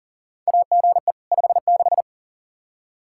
Morse code, heard as a keyed tone, about 40 wpm, 700 Hz. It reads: AGE 56
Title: K1USN SST QSO Practice 40wpm